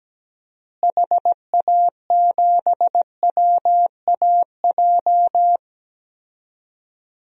Morse code, heard as a keyed tone, about 17 wpm, 700 Hz. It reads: HA7WAJ
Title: DXCC-100 Call Signs - 6 Characters - Repeated 2x - 17wpm